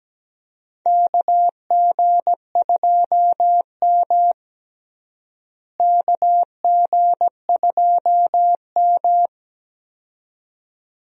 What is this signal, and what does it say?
Morse code, heard as a keyed tone, about 17 wpm, 700 Hz. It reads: KG2M KG2M
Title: USA Call Signs - 4 Characters - Repeated 2x - 17wpm